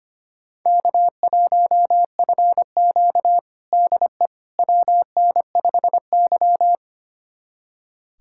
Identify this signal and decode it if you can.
Morse code, heard as a keyed tone, about 25 wpm, 700 Hz. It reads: K1FQ DE WN5Y
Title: USA Callsign DE Callsign - 4 Characters - 25wpm